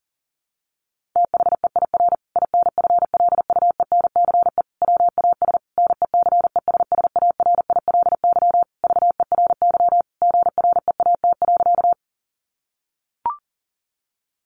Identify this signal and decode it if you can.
Morse code, heard as a keyed tone, about 40 wpm, 700 Hz. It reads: THEIR INFLUENCE WAS NECESSARILY VERY GREAT.
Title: Sentences from Top 1400 Words 40wpm